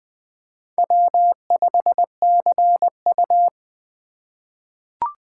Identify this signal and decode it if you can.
Morse code, heard as a keyed tone, about 20 wpm, 700 Hz. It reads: W5CU
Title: USA Call Signs - 4 Characters - 3x Character Spacing - 20wpm